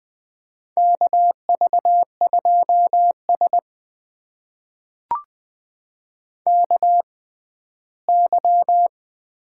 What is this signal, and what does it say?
Morse code, heard as a keyed tone, about 21 wpm, spaced as for 14 wpm, 700 Hz. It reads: KV2S KY
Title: USA Call Signs - 4 Characters - 6x Character Spacing - 20wpm